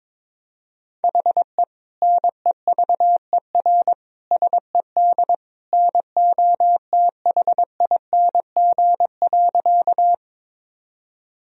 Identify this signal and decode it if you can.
Morse code, heard as a keyed tone, about 22 wpm, 700 Hz. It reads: HE NEVER SED NOTHING.
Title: Sentences from Top 500 Words and Abbreviations 22wpm